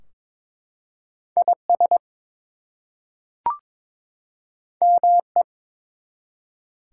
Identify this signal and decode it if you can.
Morse code, heard as a keyed tone, about 22 wpm, 700 Hz. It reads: IS ME
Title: Two Letter Words - 22wpm